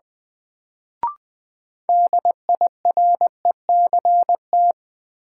Morse code, heard as a keyed tone, about 20 wpm, 700 Hz. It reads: DIRECT
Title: Sentences from Top 400 Words - New Word Drill - 20wpm